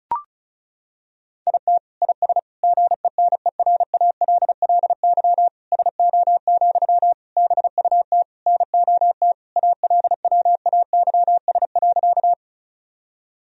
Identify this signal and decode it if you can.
Morse code, heard as a keyed tone, about 36 wpm, 700 Hz. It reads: IT IS GENERALLY SO, BUT NOT ALWAYS.
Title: Sentences from Top 1000 Words - Review Part 4 - 35wpm